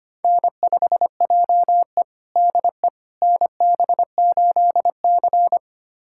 Morse code, heard as a keyed tone, about 25 wpm, 700 Hz. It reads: N5JE DE NB8C
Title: USA Callsign DE Callsign - 4 Characters - 25wpm